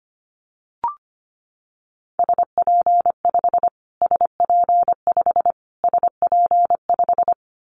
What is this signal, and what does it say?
Morse code, heard as a keyed tone, about 25 wpm, 700 Hz. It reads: SP5 SP5 SP5
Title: Callsign Prefixes for Top 25 DXCC - Repeated 3x - 25wpm